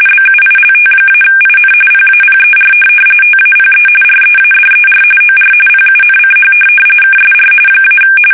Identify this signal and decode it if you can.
RTTY, about 45.45 baud. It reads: BUILT-IN PAPER TAPE READER OR PUNCH KSR -
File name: fsk850 / teletype-corp